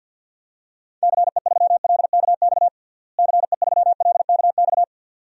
Morse code, heard as a keyed tone, about 49 wpm, 700 Hz. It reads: CE3LKX CE3LKX
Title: DXCC-100 Call Signs - 6 Characters - Repeated 2x - 50wpm